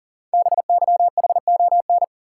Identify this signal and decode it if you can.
Morse code, heard as a keyed tone, about 40 wpm, 690 Hz. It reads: BYHON
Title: Code Groups - 5 Letters - 40wpm